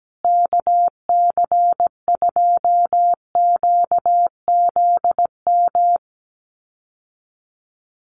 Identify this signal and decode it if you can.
Morse code, heard as a keyed tone, about 17 wpm, 700 Hz. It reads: KC2QZM
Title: USA Call Signs Repeated 2x 17wpm